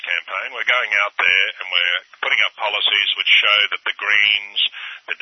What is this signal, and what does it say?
Emergency Alert System (SAME) tones, heard as harsh buzzing bursts, with speech. It is heard from 0.5 to 4.2 s.